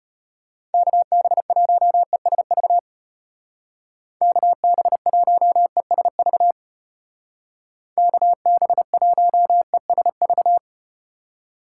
Morse code, heard as a keyed tone, about 32 wpm, 700 Hz. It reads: KB1ESV KB1ESV KB1ESV